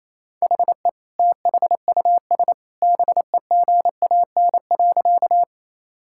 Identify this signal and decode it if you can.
Morse code, heard as a keyed tone, about 28 wpm, 700 Hz. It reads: HE THUS BEGAN.
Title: Sentences from Top 1000 Words - Review Part 5 - 28wpm